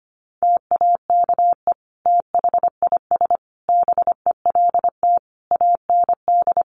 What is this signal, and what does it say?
Morse code, heard as a keyed tone, about 25 wpm, 700 Hz. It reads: TAKE THIS BELT AND
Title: Sentences from Top 1400 Words 25wpm